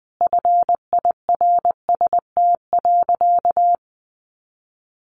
Morse code, heard as a keyed tone, about 19 wpm, 700 Hz. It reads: FIRST.
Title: Sentences from Top 1100 Words 20wpm